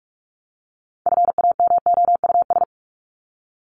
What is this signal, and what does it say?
Morse code, heard as a keyed tone, about 44 wpm, 700 Hz. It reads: FAMOUS
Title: Top 1251-1500 Words 45wpm